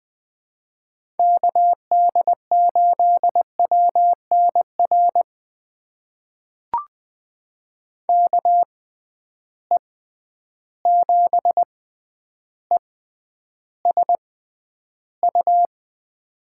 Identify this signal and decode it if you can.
Morse code, heard as a keyed tone, about 20 wpm, spaced as for 10 wpm, 700 Hz. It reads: KD8WNR KE7ESU
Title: USA Call Signs - 6 Characters - 6x Character Spacing - 20wpm